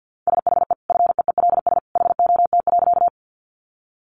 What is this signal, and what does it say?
Morse code, heard as a keyed tone, about 50 wpm, 700 Hz. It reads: SHE FEELS HOT.
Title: Additional Easy Phrases 50wpm